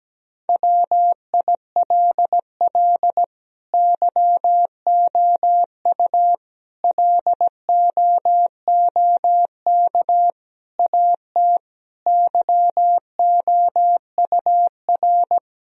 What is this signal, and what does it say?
Morse code, heard as a keyed tone, about 17 wpm, 700 Hz. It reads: WILL YOU LOOK AT YOUR
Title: Sentences from Top 1000 Words - Review Part 1 - 17wpm